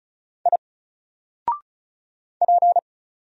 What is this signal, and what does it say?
Morse code, heard as a keyed tone, about 36 wpm, 700 Hz. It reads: I P
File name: Single-Letter-Number-35wpm